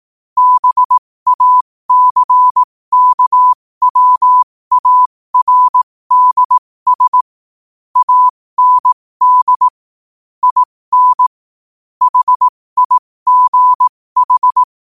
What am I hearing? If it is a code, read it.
Morse code, heard as a keyed tone, about 18 wpm, spaced as for 15 wpm, 1000 Hz. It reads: BACKWARDS AND IN HIGH
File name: qotdfiles / 15wpm